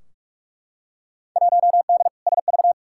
Morse code, heard as a keyed tone, about 44 wpm, 700 Hz. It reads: 1D SV
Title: Field Day Exchanges - 45wpm